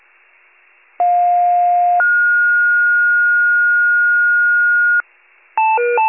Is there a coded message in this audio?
Yes: two-tone page from 1.0 to 5.0 s, heard as a one-second tone, then a three-second tone, about 710 then 1430 Hz.